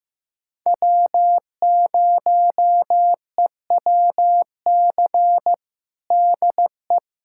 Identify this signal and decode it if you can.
Morse code, heard as a keyed tone, about 15 wpm, 700 Hz. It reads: W0EWC DE
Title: USA Callsign DE Callsign 15wpm